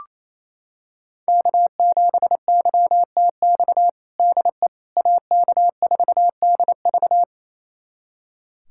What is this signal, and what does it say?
Morse code, heard as a keyed tone, about 28 wpm, 700 Hz. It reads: K7YTX DE AK4DV
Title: USA Callsign DE Callsign - 5 Characters - 28wpm